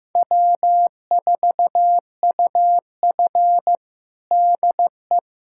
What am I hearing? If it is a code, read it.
Morse code, heard as a keyed tone, about 15 wpm, 700 Hz. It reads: W4UF DE
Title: USA Callsign DE Callsign - 4 Characters - 15wpm